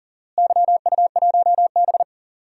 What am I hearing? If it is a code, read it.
Morse code, heard as a keyed tone, about 40 wpm, 700 Hz. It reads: YU1B
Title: DXCC-100 Call Signs - 4 Characters - Repeated 3x - 40wpm